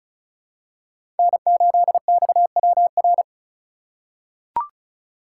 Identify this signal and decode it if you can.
Morse code, heard as a keyed tone, about 36 wpm, 700 Hz. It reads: N8XWR
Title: USA Call Signs - 2x Character Spacing - 35wpm